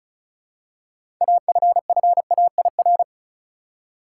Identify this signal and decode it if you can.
Morse code, heard as a keyed tone, about 36 wpm, 700 Hz. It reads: AFFAIR
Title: Six Letter Words - 35wpm